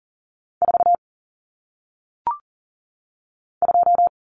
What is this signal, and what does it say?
Morse code, heard as a keyed tone, about 40 wpm, 700 Hz. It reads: 4 2
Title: Common Cut Numbers - 1-Digit 40wpm